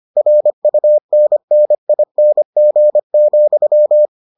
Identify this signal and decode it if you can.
Morse code, heard as a keyed tone, about 25 wpm, 600 Hz. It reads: RUNNING,